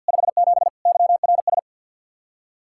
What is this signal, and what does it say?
Morse code, heard as a keyed tone, about 65 wpm, 700 Hz. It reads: 57 YRS